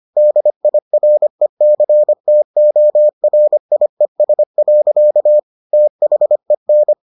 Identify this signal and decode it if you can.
Morse code, heard as a keyed tone, about 25 wpm, 600 Hz. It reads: DIRECTORIES. THEN